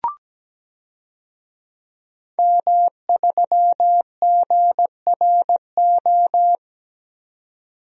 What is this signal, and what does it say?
Morse code, heard as a keyed tone, about 17 wpm, 700 Hz. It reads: M3GRO
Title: DXCC-100 Call Signs - 5 Characters - 17wpm